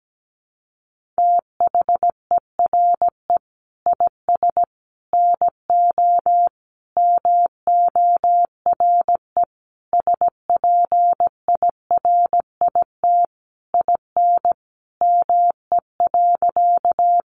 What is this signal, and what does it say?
Morse code, heard as a keyed tone, about 17 wpm, 700 Hz. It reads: THERE IS NO MORE SPIRIT IN ME.
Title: Sentences from Top 1400 Words 17wpm